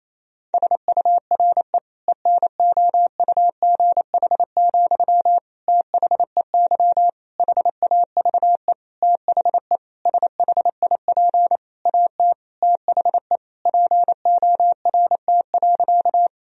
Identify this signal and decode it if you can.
Morse code, heard as a keyed tone, about 28 wpm, 700 Hz. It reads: SURE ENOUGH, THEY HAVE THE SHIP AT THE PORT.
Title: Sentences from Top 1000 Words - Review Part 2 - 28wpm